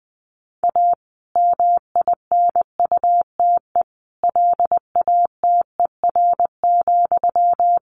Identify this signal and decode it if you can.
Morse code, heard as a keyed tone, about 20 wpm, 700 Hz. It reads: A MINUTE LATER,
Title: Sentences from Top 1500 Words 20wpm